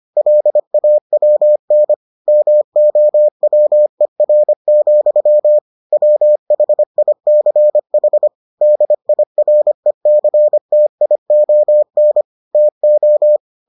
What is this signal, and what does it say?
Morse code, heard as a keyed tone, about 25 wpm, 600 Hz. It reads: LAWN MOWER, WHICH DIRECTION TO